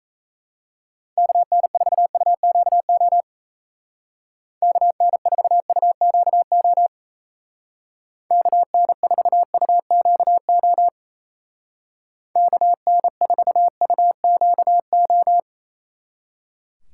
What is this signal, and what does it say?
Morse code, heard as a keyed tone, about 34 wpm, 700 Hz. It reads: KN4UQO KN4UQO KN4UQO KN4UQO
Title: Speed Racing - USA 2x3 Call Signs 28wpm